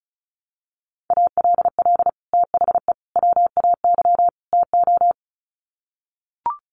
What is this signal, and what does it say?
Morse code, heard as a keyed tone, about 34 wpm, 700 Hz. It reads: ALL THE WAY TO
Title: Sets of 4 Words 35wpm